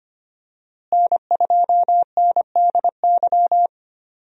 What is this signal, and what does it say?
Morse code, heard as a keyed tone, about 25 wpm, 700 Hz. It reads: N2NDY
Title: USA Call Signs - 5 Characters - Repeated 2x - 25wpm